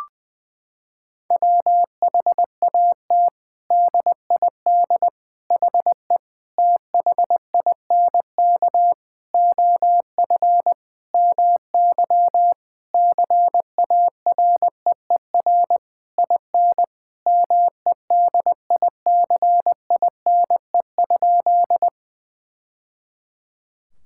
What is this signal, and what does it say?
Morse code, heard as a keyed tone, about 20 wpm, 700 Hz. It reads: WHAT DID HE THINK OF MY CAREER IN MEDICINE?
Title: Sentences from Top 1000 Words - Review Part 6 - 20wpm